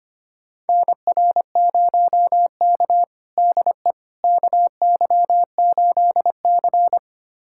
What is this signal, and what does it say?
Morse code, heard as a keyed tone, about 25 wpm, 700 Hz. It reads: NR0K DE KY8C
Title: USA Callsign DE Callsign - 4 Characters - 25wpm